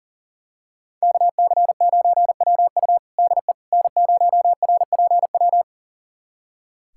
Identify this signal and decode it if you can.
Morse code, heard as a keyed tone, about 40 wpm, 700 Hz. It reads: KC9WU DE N0RPW
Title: USA Callsign DE Callsign - 5 Characters - 40wpm